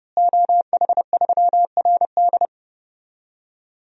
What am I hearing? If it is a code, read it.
Morse code, heard as a keyed tone, about 31 wpm, 700 Hz. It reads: OH3RD
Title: DXCC-100 Call Signs - 5 Characters - Repeated 3x - 30wpm